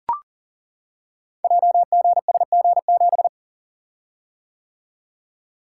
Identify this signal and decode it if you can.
Morse code, heard as a keyed tone, about 38 wpm, 700 Hz. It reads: JGSG7
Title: Code Groups - 5 Letter:Numbers - 40wpm